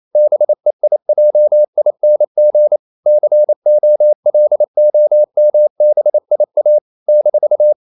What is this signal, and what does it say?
Morse code, heard as a keyed tone, about 28 wpm, 600 Hz. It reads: BEIJING COLOMBIA -
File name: CW / Practice / NationalCapitals28